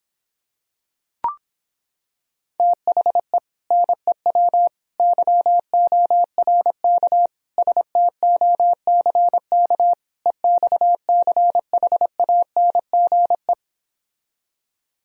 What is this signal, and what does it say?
Morse code, heard as a keyed tone, about 26 wpm, 700 Hz. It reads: THE NEW YORK STOCK EXCHANGE